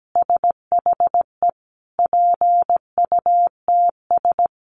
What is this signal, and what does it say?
Morse code, heard as a keyed tone, about 17 wpm, 700 Hz. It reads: SHE PUTS
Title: Easy Phrases 17wpm